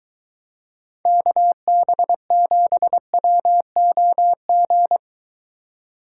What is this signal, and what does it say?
Morse code, heard as a keyed tone, about 20 wpm, 700 Hz. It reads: KB7WOG